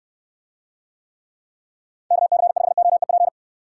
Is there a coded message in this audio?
Yes: Morse code from 2.1 to 3.3 s, heard as a keyed tone, about 65 wpm, 700 Hz. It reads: KC5OEP